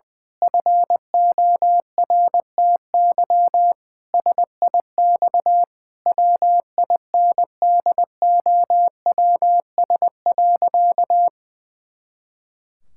Morse code, heard as a keyed tone, about 21 wpm, 700 Hz. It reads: FORTY SIX WINDOWS.
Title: Pangram Challenge v2 - 20wpm